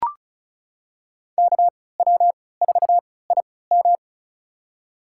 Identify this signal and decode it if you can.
Morse code, heard as a keyed tone, about 34 wpm, spaced as for 20 wpm, 700 Hz. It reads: KW4IM
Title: USA Call Signs - 5 Characters - 3x Character Spacing - 35wpm